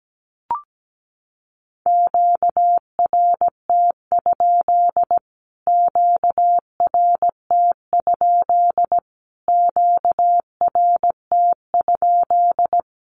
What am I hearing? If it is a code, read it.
Morse code, heard as a keyed tone, about 17 wpm, 700 Hz. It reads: QRT? QRT? QRT?
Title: Q-Codes Repeated 3x 17wpm